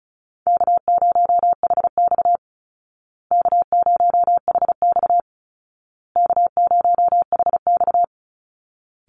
Morse code, heard as a keyed tone, about 34 wpm, 700 Hz. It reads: K0HX K0HX K0HX
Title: Call Signs Repeated 2x - 35wpm